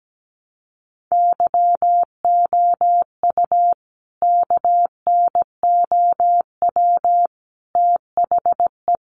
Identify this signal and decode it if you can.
Morse code, heard as a keyed tone, about 17 wpm, 700 Hz. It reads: YOU KNOW THE
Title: Sentences from Top 1000 Words 17wpm